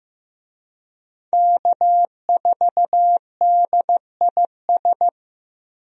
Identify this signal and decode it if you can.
Morse code, heard as a keyed tone, about 15 wpm, 700 Hz. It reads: K4DIS
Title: USA Call Signs - 5 Characters - 15wpm